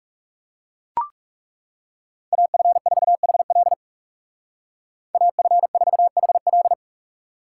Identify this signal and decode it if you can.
Morse code, heard as a keyed tone, about 39 wpm, 700 Hz. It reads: AF4HL AF4HL